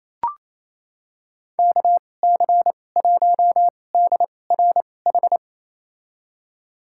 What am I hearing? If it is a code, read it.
Morse code, heard as a keyed tone, about 28 wpm, spaced as for 20 wpm, 700 Hz. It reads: KC1DRH
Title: USA Call Signs - 6 Characters - 2x Character Spacing - 28wpm